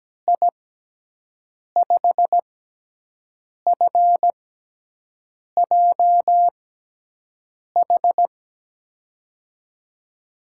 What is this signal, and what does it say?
Morse code, heard as a keyed tone, about 17 wpm, spaced as for 6 wpm, 700 Hz. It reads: I5FJH
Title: USA Call Signs - 6 Characters - 6x Character Spacing - 17wpm